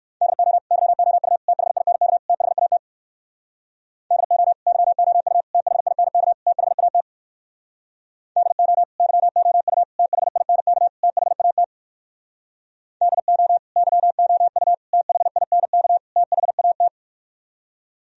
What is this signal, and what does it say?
Morse code, heard as a keyed tone, about 58 wpm, 700 Hz. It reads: DO YOU THINK THAT DO YOU THINK THAT DO YOU THINK THAT DO YOU THINK THAT